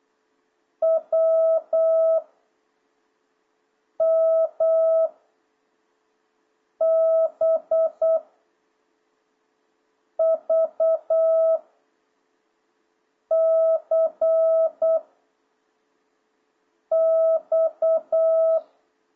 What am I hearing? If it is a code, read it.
Morse code, heard as a keyed tone, about 8 wpm, 650 Hz. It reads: W M B V C X